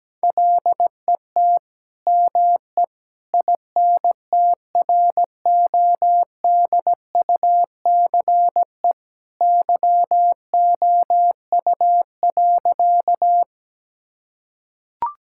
Morse code, heard as a keyed tone, about 17 wpm, 700 Hz. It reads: LET ME INTRODUCE YOU.
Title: Sentences from Top 1600 Words 17wpm